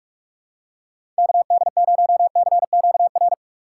Morse code, heard as a keyed tone, about 46 wpm, 700 Hz. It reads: KD0CQR
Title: USA Call Signs - 6 Characters - 4x Character Spacing - 45wpm